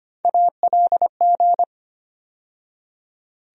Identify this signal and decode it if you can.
Morse code, heard as a keyed tone, about 25 wpm, 700 Hz. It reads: ALG
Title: USA Call Sign Postfixes - 3 Letters 25wpm